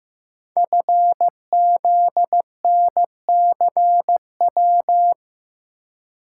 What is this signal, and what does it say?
Morse code, heard as a keyed tone, about 15 wpm, 700 Hz. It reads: FZNCW
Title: Code Groups - 5 Letters - 15wpm